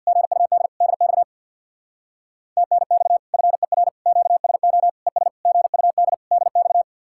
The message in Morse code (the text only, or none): GUD DX TNX FER QSO ES GUD DX